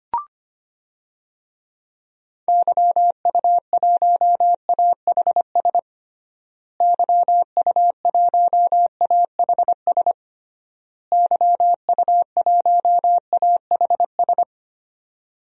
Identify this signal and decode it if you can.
Morse code, heard as a keyed tone, about 25 wpm, 700 Hz. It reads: YU1AHS YU1AHS YU1AHS
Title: DXCC-100 Call Signs - 6 Characters - Repeated 3x - 25wpm